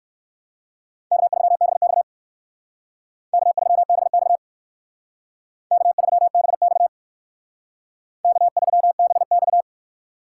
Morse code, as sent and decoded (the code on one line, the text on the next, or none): -.- ...-- -... -..- / -.- ...-- -... -..- / -.- ...-- -... -..- / -.- ...-- -... -..-
K3BX K3BX K3BX K3BX